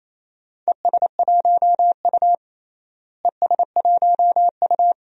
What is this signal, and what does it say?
Morse code, heard as a keyed tone, about 28 wpm, 700 Hz. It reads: ES1U ES1U
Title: DXCC-100 Call Signs - 4 Characters - Repeated 2x - 28wpm